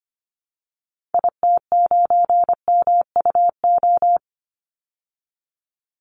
Morse code, heard as a keyed tone, about 25 wpm, 700 Hz. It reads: IT9MUO
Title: Call Signs Repeated 2x - 25wpm